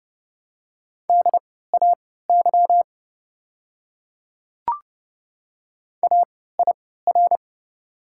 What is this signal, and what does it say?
Morse code, heard as a keyed tone, about 30 wpm, spaced as for 17 wpm, 700 Hz. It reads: DAY AIR
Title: 3x Character Spacing - Top 200 Words - 30wpm